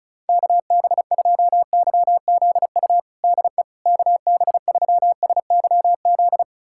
Morse code, heard as a keyed tone, about 36 wpm, 700 Hz. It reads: KB2YZU DE KB3SYZ
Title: USA Callsign DE Callsign 35wpm